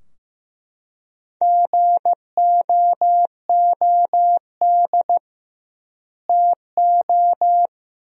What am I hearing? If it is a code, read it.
Morse code, heard as a keyed tone, about 16 wpm, 700 Hz. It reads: GOOD TO
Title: 2x Word-Spacing - Sentences from Top 500 Words - 15wpm